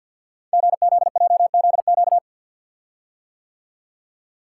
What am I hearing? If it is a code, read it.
Morse code, heard as a keyed tone, about 50 wpm, 700 Hz. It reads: GZJZQ
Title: Code Groups - 5 Letter:Numbers - 50wpm